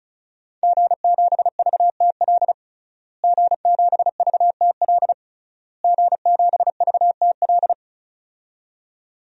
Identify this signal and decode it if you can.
Morse code, heard as a keyed tone, about 36 wpm, 700 Hz. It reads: G7VTL G7VTL G7VTL
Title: DXCC-100 Call Signs - 5 Characters - Repeated 3x - 35wpm